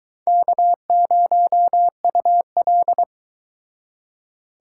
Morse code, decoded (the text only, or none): K0UL